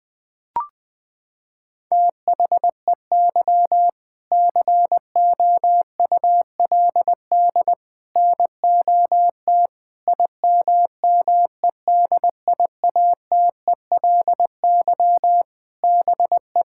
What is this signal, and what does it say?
Morse code, heard as a keyed tone, about 20 wpm, 700 Hz. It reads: THEY COULD NOT IMMEDIATELY BE
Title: Sentences with 7-Word Phrases - 20wpm